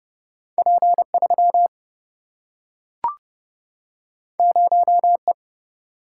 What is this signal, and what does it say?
Morse code, heard as a keyed tone, about 30 wpm, 700 Hz. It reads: P3 0E
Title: Single-Letter Single-Number 30wpm